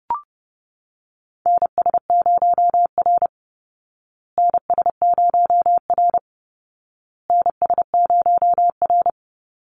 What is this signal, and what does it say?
Morse code, heard as a keyed tone, about 30 wpm, 700 Hz. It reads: NS0R NS0R NS0R
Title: Call Signs Repeated 2x - 30wpm